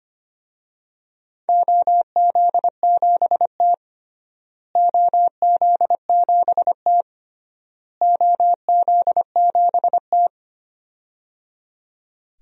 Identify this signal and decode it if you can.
Morse code, heard as a keyed tone, about 25 wpm, 700 Hz. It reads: OZ7T OZ7T OZ7T
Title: DXCC-100 Call Signs - 4 Characters - Repeated 3x - 25wpm